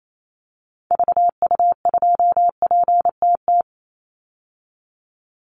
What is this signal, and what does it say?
Morse code, heard as a keyed tone, about 28 wpm, 700 Hz. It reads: VU2PTT
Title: Call Signs - 28wpm